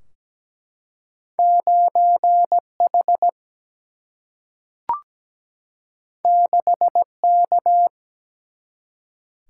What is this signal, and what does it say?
Morse code, heard as a keyed tone, about 17 wpm, 700 Hz. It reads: 9 H 6 K
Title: Single-Letter Single-Number 17wpm